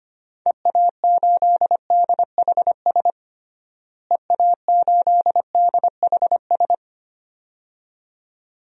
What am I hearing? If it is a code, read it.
Morse code, heard as a keyed tone, about 25 wpm, 700 Hz. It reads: EA8DHS EA8DHS
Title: DXCC-100 Call Signs - 6 Characters - Repeated 2x - 25wpm